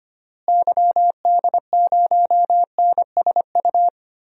Morse code, heard as a keyed tone, about 25 wpm, 700 Hz. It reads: YD0NSU
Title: DXCC-100 Call Signs - 6 Characters - Repeated 3x - 25wpm